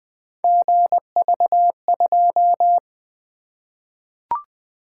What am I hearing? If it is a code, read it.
Morse code, heard as a keyed tone, about 20 wpm, 700 Hz. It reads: GV2
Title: Three Letter-Number Combinations - 20wpm